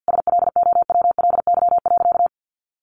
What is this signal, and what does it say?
Morse code, heard as a keyed tone, about 50 wpm, 700 Hz. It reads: SLOWLY.
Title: Sentences from Top 1100 Words 50wpm